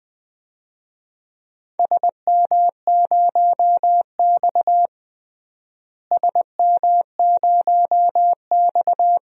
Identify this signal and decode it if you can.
Morse code, heard as a keyed tone, about 20 wpm, 700 Hz. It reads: SM0X SM0X
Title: DXCC-100 Call Signs - 4 Characters - Repeated 2x - 20wpm